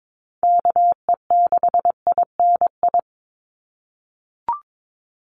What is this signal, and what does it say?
Morse code, heard as a keyed tone, about 22 wpm, 700 Hz. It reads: KE6INI
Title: USA Call Signs 22wpm